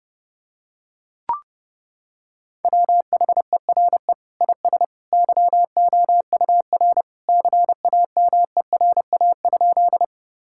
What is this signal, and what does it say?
Morse code, heard as a keyed tone, about 32 wpm, 700 Hz. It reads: WHERE IS YOUR CAMERA?
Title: Sentences from Top 1000 Words - Review Part 5 - 30wpm